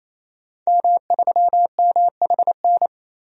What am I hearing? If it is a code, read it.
Morse code, heard as a keyed tone, about 28 wpm, 700 Hz. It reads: M3MHN
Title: DXCC-100 Call Signs - 5 Characters - 28wpm